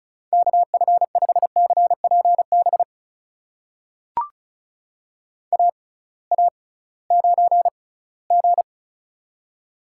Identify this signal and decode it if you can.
Morse code, heard as a keyed tone, about 36 wpm, spaced as for 20 wpm, 700 Hz. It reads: KF5CPB AA9G
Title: USA Call Signs - 6x Character Spacing - 35wpm